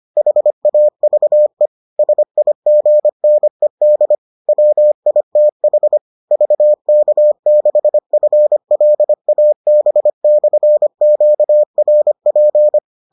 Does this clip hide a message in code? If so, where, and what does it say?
Yes: Morse code from 0.2 to 12.8 s, heard as a keyed tone, about 25 wpm, 600 Hz. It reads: HAVE SIGNED WITH VK6FLAB/QRP